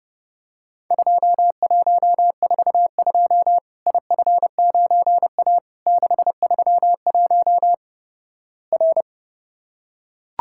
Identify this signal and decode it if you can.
Morse code, heard as a keyed tone, about 31 wpm, 700 Hz. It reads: 2142 IF9A 631 R
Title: Worked All Europe DX Contest - QTC Exchange - 30wpm